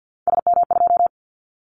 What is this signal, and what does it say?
Morse code, heard as a keyed tone, about 50 wpm, 700 Hz. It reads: SM2
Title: Callsign Prefixes for Top 25 DXCC - 50wpm